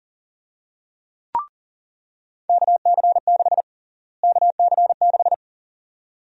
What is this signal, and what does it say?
Morse code, heard as a keyed tone, about 37 wpm, 700 Hz. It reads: KC6 KC6
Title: USA Call Sign Prefixes - Repeated 2x - 40wpm